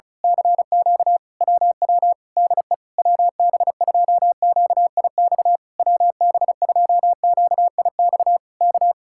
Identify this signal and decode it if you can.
Morse code, heard as a keyed tone, about 32 wpm, 700 Hz. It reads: CQ WW DE WB2QIX WB2QIX K
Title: Speed-Racing Common QSO Phrases Encore - 35wpm